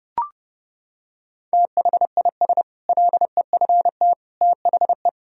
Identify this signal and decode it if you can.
Morse code, heard as a keyed tone, about 32 wpm, 700 Hz. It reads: THIS LEFT THE
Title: Sentences from Top 1000 Words - Review Part 4 - 30wpm